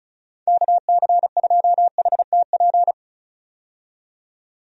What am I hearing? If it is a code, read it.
Morse code, heard as a keyed tone, about 34 wpm, 700 Hz. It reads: KC2HTP
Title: USA Call Signs - 6 Characters - 35wpm